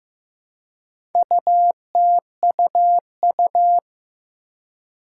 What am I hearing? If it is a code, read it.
Morse code, heard as a keyed tone, about 15 wpm, 700 Hz. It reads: UTUU
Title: Four Challenging Letter-Number Combinations 15wpm